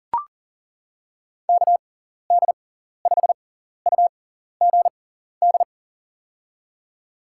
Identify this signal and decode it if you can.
Morse code, heard as a keyed tone, about 37 wpm, spaced as for 14 wpm, 700 Hz. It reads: KD5UGD
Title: USA Call Signs - 6x Character Spacing - 40wpm